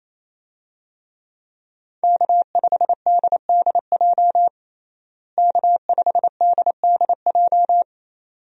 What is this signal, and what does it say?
Morse code, heard as a keyed tone, about 28 wpm, 700 Hz. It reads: K5DDJ K5DDJ
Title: USA Call Signs - 5 Characters - Repeated 2x - 28wpm